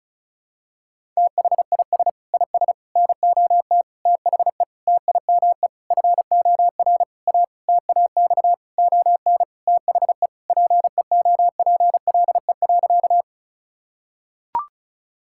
Morse code, heard as a keyed tone, about 36 wpm, 700 Hz. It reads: THIS IS NOT THE TIME FOR A TAX ON THE PEOPLE.
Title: Sentences from Top 1000 Words - Review Part 5 - 35wpm